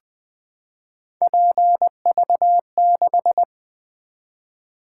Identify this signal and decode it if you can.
Morse code, heard as a keyed tone, about 20 wpm, 700 Hz. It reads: PV6
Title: Three Letter-Number Combinations - 20wpm